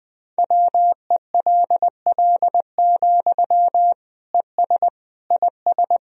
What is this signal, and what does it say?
Morse code, heard as a keyed tone, about 21 wpm, 700 Hz. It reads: WELL, ES IS
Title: Sentences from Top 200 Words and Abbreviations 20wpm